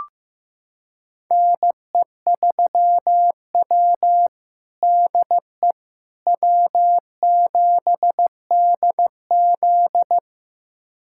Morse code, heard as a keyed tone, about 15 wpm, 700 Hz. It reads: NE3W DE W7DZ
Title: USA Callsign DE Callsign - 4 Characters - 15wpm